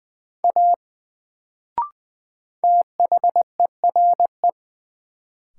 Morse code, heard as a keyed tone, about 20 wpm, 700 Hz. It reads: A THERE
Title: Sentences from Top 100 Words - New Word Drill - 20wpm